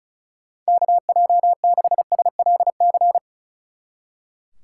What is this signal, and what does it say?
Morse code, heard as a keyed tone, about 46 wpm, 700 Hz. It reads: KJ6SLC